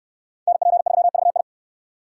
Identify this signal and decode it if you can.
Morse code, heard as a keyed tone, about 63 wpm, 690 Hz. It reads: NP3VI